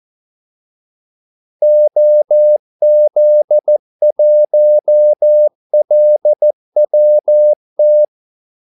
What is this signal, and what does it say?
Morse code, heard as a keyed tone, about 14 wpm, 600 Hz. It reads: OZ1LWT
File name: Callsigns-14wpm